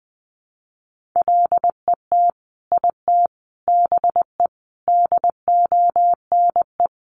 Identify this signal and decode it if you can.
Morse code, heard as a keyed tone, about 20 wpm, 700 Hz. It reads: LET IT BE DONE
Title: Sentences from Top 1100 Words 20wpm